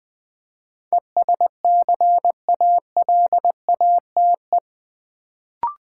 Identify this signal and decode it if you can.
Morse code, heard as a keyed tone, about 20 wpm, 700 Hz. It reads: ESCALATE
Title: Build A Word - 4 Letters At A Time - 8-Letter Words - 20wpm